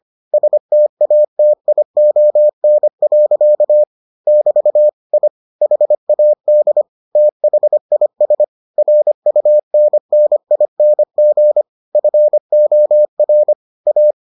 Morse code, heard as a keyed tone, about 25 wpm, 600 Hz. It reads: STATION. <BT> I HAD THIS RUNNING FOR A